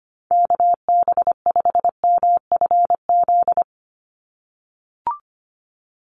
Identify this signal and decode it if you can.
Morse code, heard as a keyed tone, about 25 wpm, 700 Hz. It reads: KB5MFZ
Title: USA Call Signs 25wpm